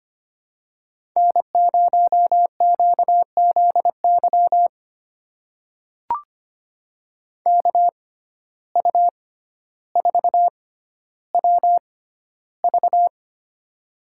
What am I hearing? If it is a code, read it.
Morse code, heard as a keyed tone, about 25 wpm, spaced as for 13 wpm, 700 Hz. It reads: N0QZY KU4WV
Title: USA Call Signs - 5 Characters - 6x Character Spacing - 25wpm